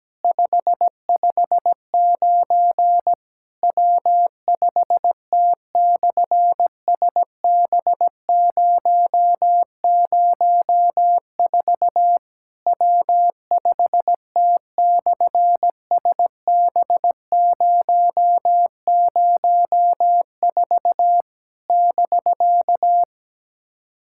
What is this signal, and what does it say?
Morse code, heard as a keyed tone, about 17 wpm, 700 Hz. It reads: 559 W5T/SB004 W5T/SB004 <BK>
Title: SOTA QSO Practice 17wpm